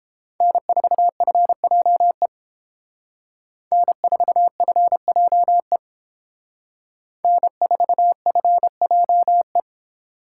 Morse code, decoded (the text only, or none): N4FJE N4FJE N4FJE